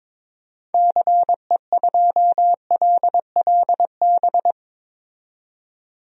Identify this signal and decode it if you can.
Morse code, heard as a keyed tone, about 22 wpm, 700 Hz. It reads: CE2LLB
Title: DXCC-100 Call Signs - 6 Characters - 22wpm